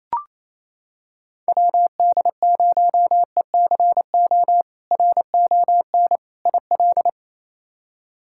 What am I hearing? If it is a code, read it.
Morse code, heard as a keyed tone, about 28 wpm, 700 Hz. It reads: WD0ECO RON IL
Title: K1USN SST QSO Practice - 28wpm